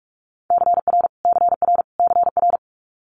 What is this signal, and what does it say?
Morse code, heard as a keyed tone, about 46 wpm, 700 Hz. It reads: CR CR CR
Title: Multi-letter Phonemes Repeated 3x 45wpm